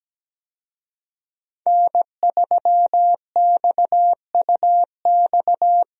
Morse code, heard as a keyed tone, about 17 wpm, 700 Hz. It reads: N3XUX
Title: USA Call Signs - 5 Characters - Repeated 2x - 17wpm